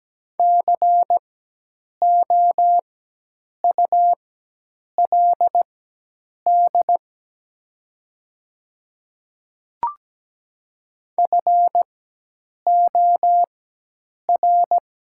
Morse code, heard as a keyed tone, about 17 wpm, spaced as for 8 wpm, 700 Hz. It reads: COULD FOR
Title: 4x Character Spacing - Top 100 Words - 17wpm